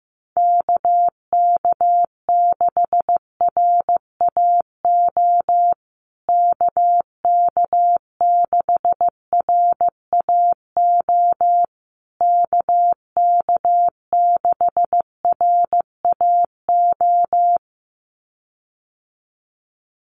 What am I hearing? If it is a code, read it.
Morse code, heard as a keyed tone, about 15 wpm, 700 Hz. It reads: KK6RAO KK6RAO KK6RAO
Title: USA Call Signs Repeated 3x 15wpm